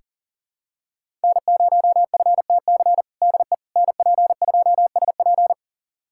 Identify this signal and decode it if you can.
Morse code, heard as a keyed tone, about 37 wpm, 700 Hz. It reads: N0FTC DE NP2SP
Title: USA Callsign DE Callsign - 5 Characters - 40wpm